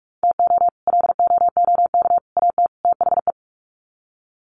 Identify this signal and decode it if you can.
Morse code, heard as a keyed tone, about 44 wpm, 700 Hz. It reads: TO LOOK AT THE
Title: Sets of 4 Words 45wpm